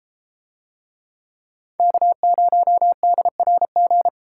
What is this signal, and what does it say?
Morse code, heard as a keyed tone, about 31 wpm, 700 Hz. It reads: K0DRG